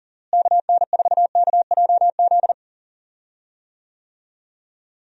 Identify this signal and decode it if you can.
Morse code, heard as a keyed tone, about 40 wpm, 700 Hz. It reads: KN4KJZ
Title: USA Call Signs - 6 Characters - Repeated 2x - 40wpm